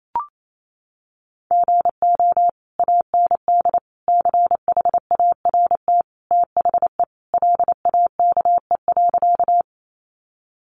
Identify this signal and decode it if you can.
Morse code, heard as a keyed tone, about 28 wpm, 700 Hz. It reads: GO AND CHART THE LAKE.
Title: Sentences from Top 1300 Words 28wpm